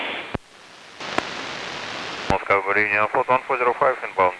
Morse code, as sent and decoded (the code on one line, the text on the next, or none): none